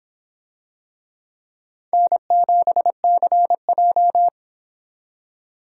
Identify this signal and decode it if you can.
Morse code, heard as a keyed tone, about 26 wpm, 700 Hz. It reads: N7CJ